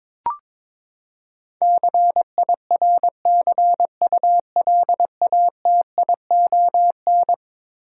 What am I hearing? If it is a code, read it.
Morse code, heard as a keyed tone, about 22 wpm, 700 Hz. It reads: CIRCULATION
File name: practice / 22wpm / Eleven Letter Words - 22wpm